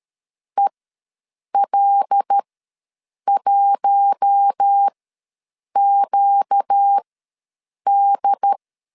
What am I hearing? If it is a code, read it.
Morse code, heard as a keyed tone, about 13 wpm, spaced as for 7 wpm, 800 Hz. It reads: EL1QD
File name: LEZIONI / 7 GF9Z / 7 G ws 12wpm 3140